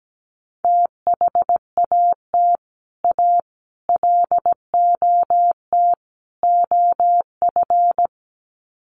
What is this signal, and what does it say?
Morse code, heard as a keyed tone, about 17 wpm, 700 Hz. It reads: THAT A LOT OF
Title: Sets of 4 Words 17wpm